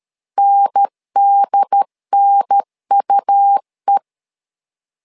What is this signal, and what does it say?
Morse code, heard as a keyed tone, about 13 wpm, 800 Hz. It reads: NDNUE
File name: LEZIONI / 2 UHDN / 2 UHDN 12wpm 1456